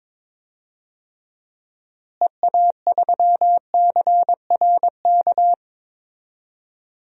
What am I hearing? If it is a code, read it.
Morse code, heard as a keyed tone, about 21 wpm, 700 Hz. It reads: EA3CRK
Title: DXCC-100 Call Signs - 6 Characters - Repeated 2x - 22wpm